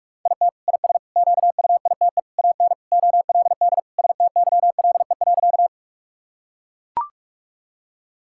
Morse code, heard as a keyed tone, about 43 wpm, 700 Hz. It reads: IT IS QUITE AN OLD STYLE.
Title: Sentences from Top 1000 Words - Review Part 5 - 45wpm